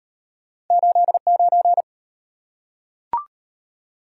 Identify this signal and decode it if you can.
Morse code, heard as a keyed tone, about 37 wpm, 700 Hz. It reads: 89